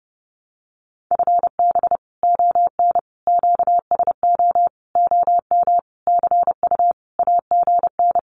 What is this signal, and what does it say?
Morse code, heard as a keyed tone, about 30 wpm, 700 Hz. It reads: FB ON QSO OM CU AGN
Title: Common QSO Phrases 30wpm